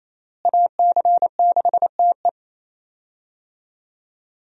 Morse code, heard as a keyed tone, about 28 wpm, 700 Hz. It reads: AC6TE
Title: USA Call Signs - 5 Characters - Repeated 2x - 28wpm